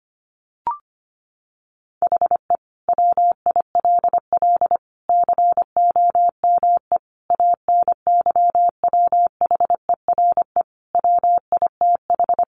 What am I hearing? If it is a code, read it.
Morse code, heard as a keyed tone, about 25 wpm, 700 Hz. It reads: HE WILL COME ANYWHERE WITH
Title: Sentences from Top 1400 Words 25wpm